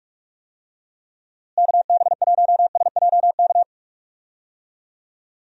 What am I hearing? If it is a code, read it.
Morse code, heard as a keyed tone, about 46 wpm, 700 Hz. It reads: KB1SJK
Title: USA Call Signs - 6 Characters - Repeated 2x - 45wpm